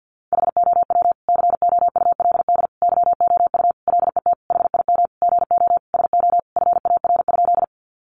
Morse code, heard as a keyed tone, about 50 wpm, 700 Hz. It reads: HOW COULD YOU LET HIM GO SO FAR?
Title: Sentences from Top 300 Words 50wpm